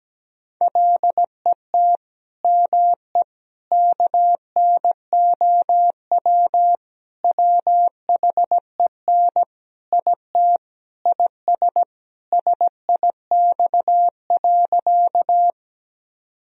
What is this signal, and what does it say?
Morse code, heard as a keyed tone, about 17 wpm, 700 Hz. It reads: LET ME KNOW WHEN IT IS SIX.
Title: Sentences from Top 1000 Words - Review Part 2 - 17wpm